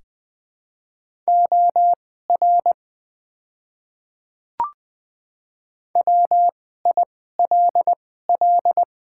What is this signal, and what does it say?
Morse code, heard as a keyed tone, about 21 wpm, spaced as for 15 wpm, 700 Hz. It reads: OR WILL
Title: 2x Character Spacing - Top 500 Words - 20wpm